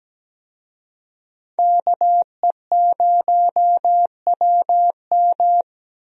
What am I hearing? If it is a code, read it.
Morse code, heard as a keyed tone, about 17 wpm, 700 Hz. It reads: KE0WM
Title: USA Call Signs - 5 Characters - Repeated 3x - 17wpm